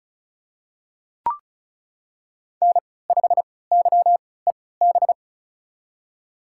Morse code, heard as a keyed tone, about 34 wpm, spaced as for 20 wpm, 700 Hz. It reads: N5YEB
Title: USA Call Signs - 3x Character Spacing - 35wpm